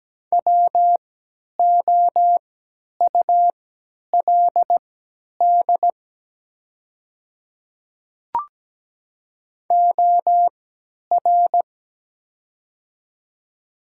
Morse code, heard as a keyed tone, about 17 wpm, spaced as for 10 wpm, 700 Hz. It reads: WOULD OR
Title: 3x Character Spacing - Top 100 Words - 17wpm